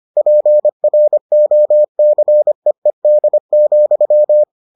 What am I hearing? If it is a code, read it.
Morse code, heard as a keyed tone, about 25 wpm, 600 Hz. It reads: PROCEED,